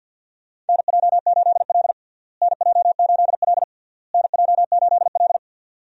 Morse code, heard as a keyed tone, about 52 wpm, 700 Hz. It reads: NJ8L NJ8L NJ8L
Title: USA Call Signs - 4 Characters - Repeated 3x - 50wpm